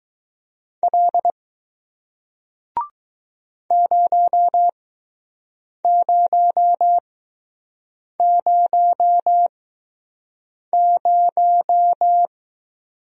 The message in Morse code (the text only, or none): L 0000